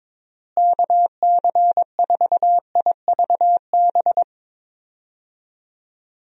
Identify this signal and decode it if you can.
Morse code, heard as a keyed tone, about 22 wpm, 700 Hz. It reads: KC4IVB
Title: USA Call Signs - 6 Characters - Repeated 3x - 22wpm